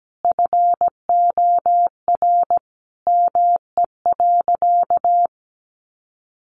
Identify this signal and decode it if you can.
Morse code, heard as a keyed tone, about 17 wpm, 700 Hz. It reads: FOR ME.
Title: Sentences from Top 300 Words 17wpm